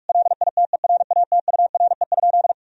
Morse code, heard as a keyed tone, about 46 wpm, 700 Hz. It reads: LITERATURE?
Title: Sentences from Top 1000 Words - Review Part 3 - 45wpm